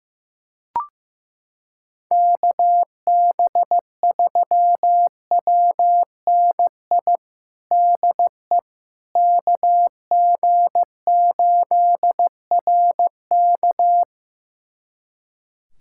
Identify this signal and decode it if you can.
Morse code, heard as a keyed tone, about 15 wpm, 700 Hz. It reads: KB3WNI DE KG8RK
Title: USA Callsign DE Callsign 15wpm